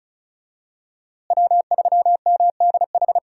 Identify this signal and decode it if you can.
Morse code, heard as a keyed tone, about 36 wpm, 700 Hz. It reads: W3MDH
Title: USA Call Signs - 5 Characters - Repeated 3x - 35wpm